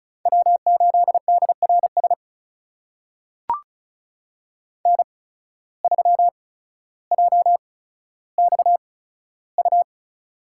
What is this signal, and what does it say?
Morse code, heard as a keyed tone, about 36 wpm, spaced as for 15 wpm, 690 Hz. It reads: W8DRS N3JXU
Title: USA Call Signs - 5 Characters - 8x Character Spacing - 35wpm